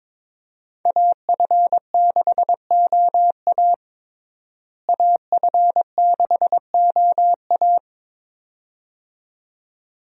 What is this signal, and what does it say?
Morse code, heard as a keyed tone, about 22 wpm, 700 Hz. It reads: AF6OA AF6OA
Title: USA Call Signs - 5 Characters - Repeated 2x - 22wpm